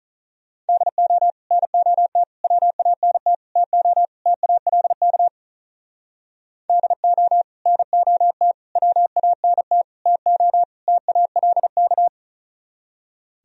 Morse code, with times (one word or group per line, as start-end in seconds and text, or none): 0.7-1.3 DO
1.5-2.2 NOT
2.4-3.3 WANT
3.6-4.0 TO
4.3-5.3 TALK
6.7-7.4 DO
7.7-8.5 NOT
8.8-9.8 WANT
10.1-10.6 TO
10.9-12.1 TALK